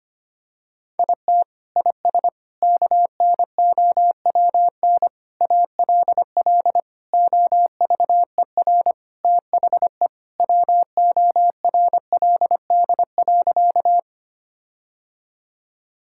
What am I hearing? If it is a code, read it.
Morse code, heard as a keyed tone, about 25 wpm, 700 Hz. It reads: IT IS KNOWN ALL OVER THE WORLD.
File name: practice / 25wpm / Sentences with 4-Word Phrases - 25wpm